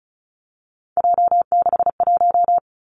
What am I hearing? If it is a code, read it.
Morse code, heard as a keyed tone, about 36 wpm, 700 Hz. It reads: J61
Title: Three Challenging Letter-Number Combinations 35wpm